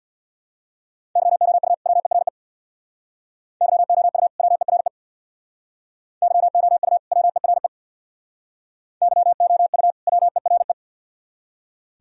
Morse code, heard as a keyed tone, about 65 wpm, 700 Hz. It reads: YOU WERE YOU WERE YOU WERE YOU WERE